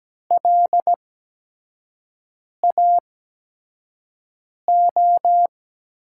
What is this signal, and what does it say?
Morse code, heard as a keyed tone, about 17 wpm, spaced as for 5 wpm, 700 Hz. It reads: LAO
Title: USA Call Signs - 6 Characters - 8x Character Spacing - 17wpm